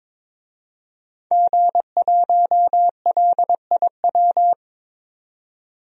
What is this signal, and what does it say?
Morse code, heard as a keyed tone, about 22 wpm, 700 Hz. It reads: G1LIW
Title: DXCC-100 Call Signs - 5 Characters - 22wpm